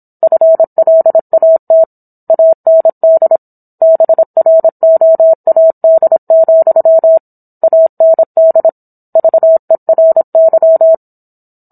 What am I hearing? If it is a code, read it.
Morse code, heard as a keyed tone, about 27 wpm, 650 Hz. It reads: FLAT AND BROAD, AND VERY